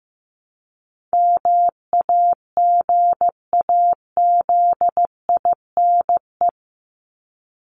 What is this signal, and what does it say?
Morse code, heard as a keyed tone, about 15 wpm, 700 Hz. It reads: MAGAZINE
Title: Top 751-1000 Words 15wpm